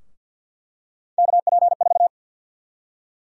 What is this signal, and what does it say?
Morse code, heard as a keyed tone, about 50 wpm, 700 Hz. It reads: KP4
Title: USA Call Sign Prefixes - 50wpm